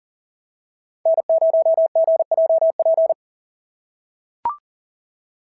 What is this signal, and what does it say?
Morse code, heard as a keyed tone, about 40 wpm, 690 Hz. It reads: N0GJP
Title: SOTA QSO Practice 40wpm